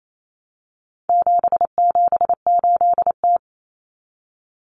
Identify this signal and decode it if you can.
Morse code, heard as a keyed tone, about 28 wpm, 700 Hz. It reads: 778T
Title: Common Cut Numbers - 4-Digit 28wpm